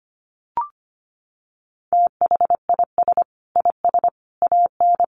Morse code, heard as a keyed tone, about 25 wpm, 700 Hz. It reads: THIS IS AN
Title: Sentences from Top 1000 Words 25wpm